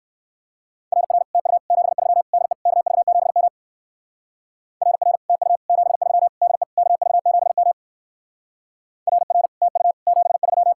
Morse code, heard as a keyed tone, about 58 wpm, 700 Hz. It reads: RR TU 73 DE KV7M RR TU 73 DE KV7M RR TU 73